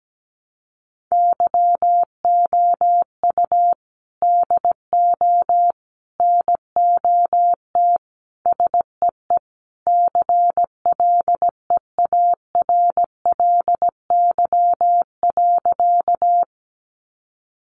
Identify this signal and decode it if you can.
Morse code, heard as a keyed tone, about 17 wpm, 700 Hz. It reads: YOU DO NOT SEE CLEARLY.
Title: Sentences from Top 1300 Words 17wpm